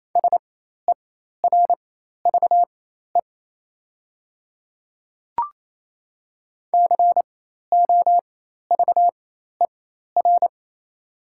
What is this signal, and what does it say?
Morse code, heard as a keyed tone, about 28 wpm, spaced as for 13 wpm, 690 Hz. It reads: SERVE COVER
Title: 4x Character Spacing - Top 500 Words - 28wpm